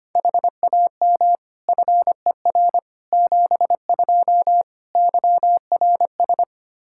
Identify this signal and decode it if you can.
Morse code, heard as a keyed tone, about 25 wpm, 700 Hz. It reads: HAM FER 72 YRS
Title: K1USN SST QSO Practice 25wpm